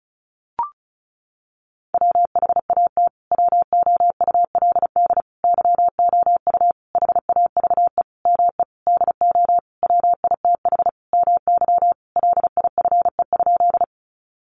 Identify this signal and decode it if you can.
Morse code, heard as a keyed tone, about 34 wpm, 700 Hz. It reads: WHAT WOULD YOU HAVE ME DO WITH MY LIFE?
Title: Sentences from Top 300 Words 35wpm